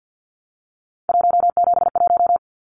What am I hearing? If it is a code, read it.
Morse code, heard as a keyed tone, about 50 wpm, 700 Hz. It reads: 171
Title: Numbers - 3-Digit 50wpm